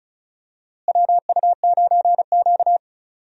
Morse code, heard as a keyed tone, about 36 wpm, 700 Hz. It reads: WU9Q
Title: USA Call Signs - 4 Characters - Repeated 2x - 35wpm